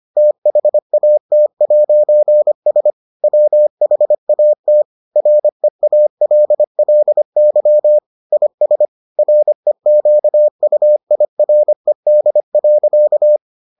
Morse code, heard as a keyed tone, about 25 wpm, 600 Hz. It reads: THAT'S WHAT REALLY IS REQUIRED.